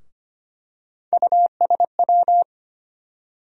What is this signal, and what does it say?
Morse code, heard as a keyed tone, about 25 wpm, 700 Hz. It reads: USW
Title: USA Call Sign Postfixes - 3 Letters 25wpm